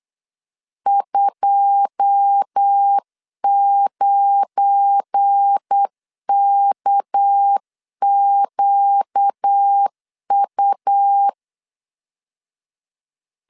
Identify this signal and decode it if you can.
Morse code, heard as a keyed tone, about 8 wpm, 800 Hz. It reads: 29KQU
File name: CW / DET / 1 9 1500C S A 8wpm 5637